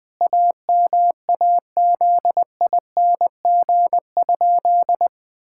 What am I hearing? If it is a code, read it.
Morse code, heard as a keyed tone, about 20 wpm, 700 Hz. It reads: AMAZING?
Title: Pangram Challenge v2 - 20wpm